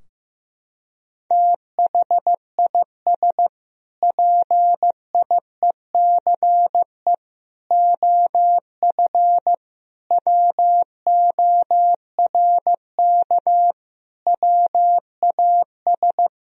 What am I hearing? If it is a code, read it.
Morse code, heard as a keyed tone, about 15 wpm, 700 Hz. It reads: THIS PIECE OF WORK WAS
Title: Sentences from Top 1000 Words - Review Part 2 - 15wpm